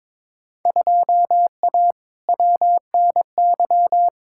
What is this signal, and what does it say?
Morse code, heard as a keyed tone, about 22 wpm, 700 Hz. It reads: 2A WNY
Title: Field Day Exchanges - 22wpm